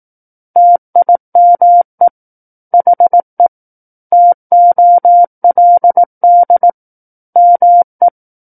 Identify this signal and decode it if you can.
Morse code, heard as a keyed tone, about 18 wpm, 700 Hz. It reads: TIME HE TOLD ME
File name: 18 WPM